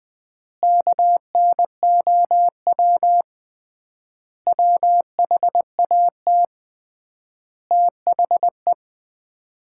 Morse code, heard as a keyed tone, about 20 wpm, 700 Hz. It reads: KNOW WHAT THE
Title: 3x Word-Spacing - Sets of 5 Words - 20wpm